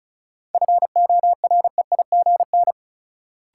Morse code, heard as a keyed tone, about 36 wpm, 700 Hz. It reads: FOREIGN
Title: Sentences from Top 1400 Words - New Word Drill Alt - 35wpm